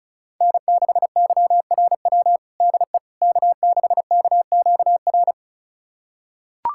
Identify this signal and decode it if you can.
Morse code, heard as a keyed tone, about 34 wpm, 700 Hz. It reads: N6YRW DE K6KQR
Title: USA Callsign DE Callsign - 5 Characters - 35wpm